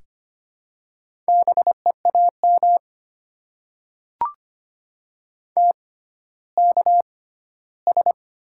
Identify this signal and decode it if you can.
Morse code, heard as a keyed tone, about 25 wpm, spaced as for 14 wpm, 700 Hz. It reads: BEAM TKS
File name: Top 90 QSO Elements - 6x Character Spacing - 25wpm